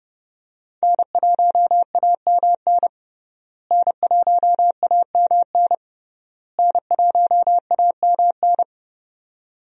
Morse code, heard as a keyed tone, about 30 wpm, 700 Hz. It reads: N1AMN N1AMN N1AMN
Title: USA Call Signs - 5 Characters - Repeated 3x - 30wpm